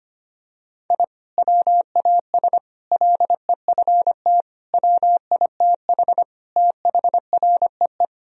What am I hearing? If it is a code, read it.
Morse code, heard as a keyed tone, about 25 wpm, 700 Hz. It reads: I WAS LEFT WITH THREE
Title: Sentences from Top 1000 Words - Review Part 1 - 25wpm